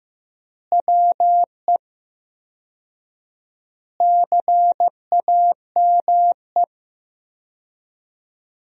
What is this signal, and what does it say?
Morse code, heard as a keyed tone, about 15 wpm, 700 Hz. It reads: WE CAME
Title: 4x Word-Spacing - Sentences from Top 400 Words - 15wpm